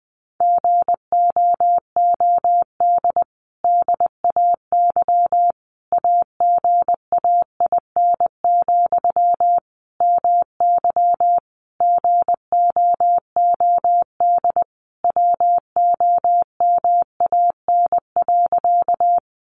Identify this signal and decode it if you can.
Morse code, heard as a keyed tone, about 20 wpm, 700 Hz. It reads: GOOD DAY AGAIN, MY GOOD WOMAN.
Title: Sentences from Top 1100 Words 20wpm